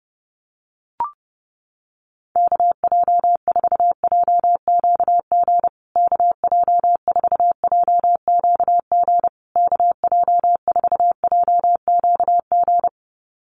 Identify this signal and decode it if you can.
Morse code, heard as a keyed tone, about 28 wpm, 700 Hz. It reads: KJ4JQG KJ4JQG KJ4JQG
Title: USA Call Signs Repeated 3x 30wpm